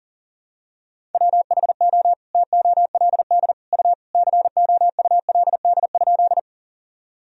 Morse code, heard as a keyed tone, about 40 wpm, 700 Hz. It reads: WHO TOLD U COULD?
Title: Sentences from Top 400 Words and Abbreviations 40wpm